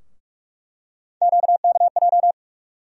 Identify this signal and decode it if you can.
Morse code, heard as a keyed tone, about 46 wpm, 700 Hz. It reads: QKJ
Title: USA Call Sign Postfixes - 3 Letters 45wpm